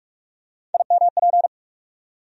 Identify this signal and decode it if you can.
Morse code, heard as a keyed tone, about 42 wpm, 700 Hz. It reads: IMP
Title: USA Call Sign Postfixes - 3 Letters 45wpm